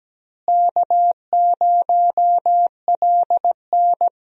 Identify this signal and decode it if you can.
Morse code, heard as a keyed tone, about 17 wpm, 700 Hz. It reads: K0LN
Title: USA Call Signs - 4 Characters - Repeated 3x - 17wpm